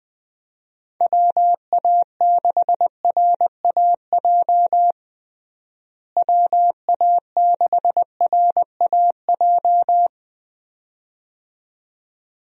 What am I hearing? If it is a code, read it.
Morse code, heard as a keyed tone, about 20 wpm, 700 Hz. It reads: WA6RAJ WA6RAJ
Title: USA Call Signs - 6 Characters - Repeated 2x - 20wpm